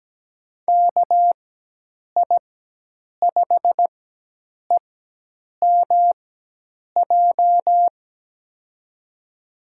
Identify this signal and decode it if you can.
Morse code, heard as a keyed tone, about 17 wpm, spaced as for 8 wpm, 700 Hz. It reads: KI5EMJ
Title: USA Call Signs - 6 Characters - 4x Character Spacing - 17wpm